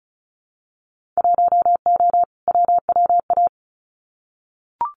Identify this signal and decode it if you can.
Morse code, heard as a keyed tone, about 34 wpm, 700 Hz. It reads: 1O WWA
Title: Winter Field Day Exchanges 35wpm